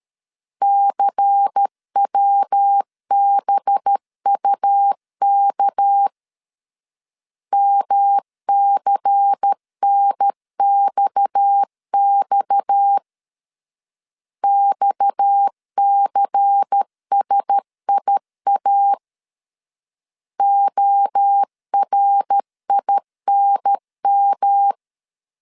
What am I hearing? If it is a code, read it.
Morse code, heard as a keyed tone, about 13 wpm, 800 Hz. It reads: CWBUK MCNXX XCSIA ORINM
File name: LEZIONI / 6 PKX0 / 6 PKX 12wpm 2024